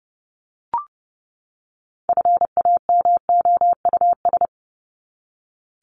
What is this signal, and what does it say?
Morse code, heard as a keyed tone, about 30 wpm, 700 Hz. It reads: FAMOUS
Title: Top 1251-1500 Words 30wpm